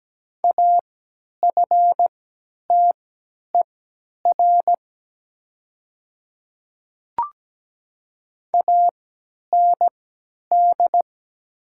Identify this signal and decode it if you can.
Morse code, heard as a keyed tone, about 17 wpm, spaced as for 10 wpm, 700 Hz. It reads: AFTER AND
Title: 3x Character Spacing - Top 300 Words - 17wpm